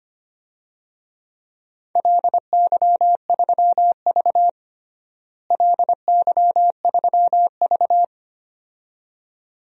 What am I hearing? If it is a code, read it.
Morse code, heard as a keyed tone, about 25 wpm, 700 Hz. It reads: LY3V LY3V
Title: DXCC-100 Call Signs - 4 Characters - Repeated 2x - 25wpm